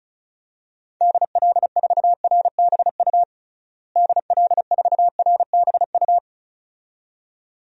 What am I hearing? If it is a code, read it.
Morse code, heard as a keyed tone, about 34 wpm, 700 Hz. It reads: DL4RBU DL4RBU
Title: DXCC-100 Call Signs - 6 Characters - Repeated 2x - 35wpm